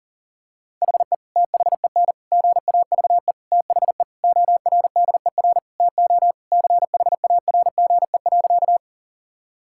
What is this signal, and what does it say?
Morse code, heard as a keyed tone, about 40 wpm, 700 Hz. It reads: HE THEN GAVE THE ORDER TO CHARGE.
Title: Sentences from Top 1000 Words - Review Part 6 - 40wpm